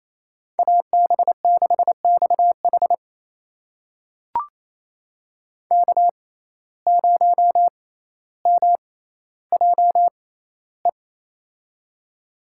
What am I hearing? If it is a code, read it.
Morse code, heard as a keyed tone, about 28 wpm, spaced as for 14 wpm, 700 Hz. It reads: AB6XH K0MJE
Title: USA Call Signs - 5 Characters - 6x Character Spacing - 28wpm